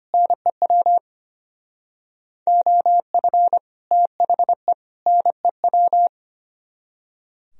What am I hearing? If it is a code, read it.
Morse code, heard as a keyed tone, about 30 wpm, 700 Hz. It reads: NEW OF THE NEW
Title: Speed-Racing - Sets of 3 Words 25wpm